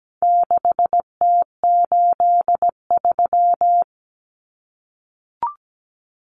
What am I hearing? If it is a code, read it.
Morse code, heard as a keyed tone, about 17 wpm, 700 Hz. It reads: 6T83
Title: Common Cut Numbers - 4-Digit 17wpm